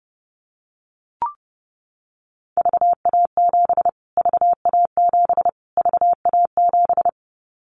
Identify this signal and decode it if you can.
Morse code, heard as a keyed tone, about 30 wpm, 700 Hz. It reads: VA7 VA7 VA7
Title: Callsign Prefixes for Top 25 DXCC - Repeated 3x - 30wpm